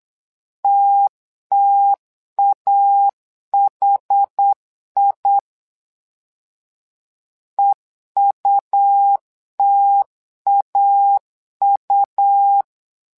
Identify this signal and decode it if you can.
Morse code, heard as a keyed tone, about 8 wpm, 800 Hz. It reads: TTAHI EUTAU
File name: LEZIONI / 2 UHDN / 2 UH 8wpm 2129